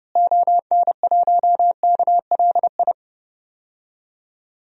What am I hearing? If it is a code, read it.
Morse code, heard as a keyed tone, about 31 wpm, 700 Hz. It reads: ON1KLI
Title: DXCC-100 Call Signs - 6 Characters - Repeated 3x - 30wpm